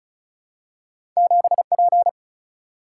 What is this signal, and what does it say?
Morse code, heard as a keyed tone, about 36 wpm, 700 Hz. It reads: 7P
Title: Single-Letter Single-Number 35wpm